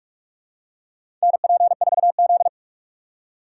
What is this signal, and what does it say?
Morse code, heard as a keyed tone, about 46 wpm, 700 Hz. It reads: NP4Z
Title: USA Call Signs - 4 Characters - 6x Character Spacing - 45wpm